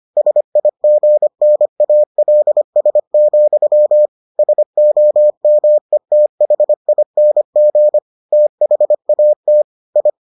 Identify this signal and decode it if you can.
Morse code, heard as a keyed tone, about 25 wpm, 600 Hz. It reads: SIGNALS, SOMETHING THAT I